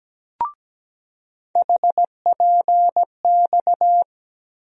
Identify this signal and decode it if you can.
Morse code, heard as a keyed tone, about 17 wpm, 700 Hz. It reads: HPX
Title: USA Call Sign Postfixes - 3 Letters 17wpm